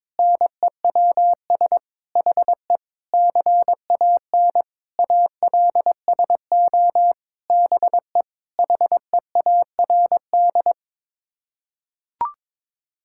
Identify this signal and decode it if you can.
Morse code, heard as a keyed tone, about 22 wpm, 700 Hz. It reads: NEWS HE CAN ALSO BE HEARD
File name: Sets of 9 Words - 22wpm